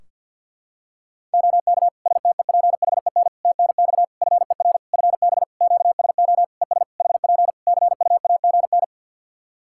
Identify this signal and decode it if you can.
Morse code, heard as a keyed tone, about 51 wpm, 700 Hz. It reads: OK STEPHEN TNX FER FB QSO ES HP CUAGN
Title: K1USN SST QSO Practice 50wpm